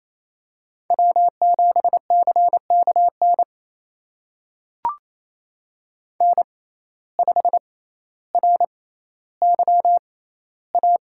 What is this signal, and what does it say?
Morse code, heard as a keyed tone, about 28 wpm, spaced as for 14 wpm, 700 Hz. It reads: W7CKN N5RYA
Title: USA Call Signs - 5 Characters - 6x Character Spacing - 28wpm